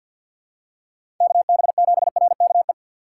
Morse code, heard as a keyed tone, about 48 wpm, 700 Hz. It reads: KB7RKE
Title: USA Call Signs - 6 Characters - 3x Character Spacing - 50wpm